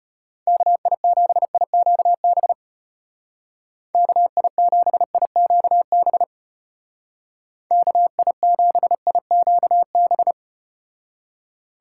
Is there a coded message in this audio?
Yes: Morse code from 0.5 to 10.3 s, heard as a keyed tone, about 34 wpm, 700 Hz. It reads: KI7IQB KI7IQB KI7IQB